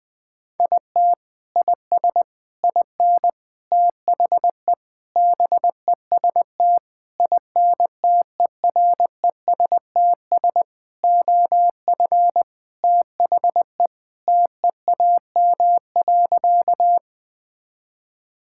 Morse code, heard as a keyed tone, about 19 wpm, 700 Hz. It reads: IT IS IN THE BEST INTERESTS OF THE TEAM.
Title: Sentences with 6-Word Phrases - 20wpm